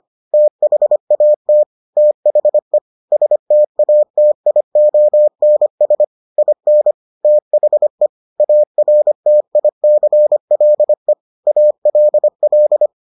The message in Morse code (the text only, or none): THAT THE STATIONS IN THE ARTICLE ALL